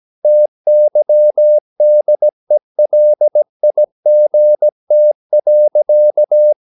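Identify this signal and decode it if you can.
Morse code, heard as a keyed tone, about 17 wpm, 600 Hz. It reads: TYDELIGT.
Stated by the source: Slow (QRS) Morse practice at 17 wpm, from CW med Gnister.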